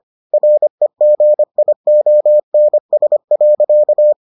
Morse code, heard as a keyed tone, about 25 wpm, 600 Hz. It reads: REGIONS.